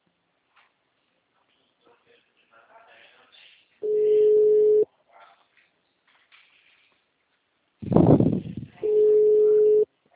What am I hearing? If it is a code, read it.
Ringback tone (Europe, 425 Hz), with speech.